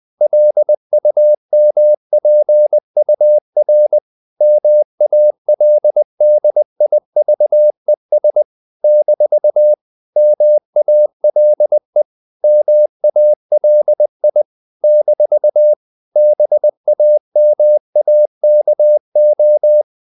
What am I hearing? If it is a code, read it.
Morse code, heard as a keyed tone, about 20 wpm, 600 Hz. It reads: LUMPUR MALDIVES - MALE MALI - BAMAKO
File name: CW / Practice / NationalCapitals20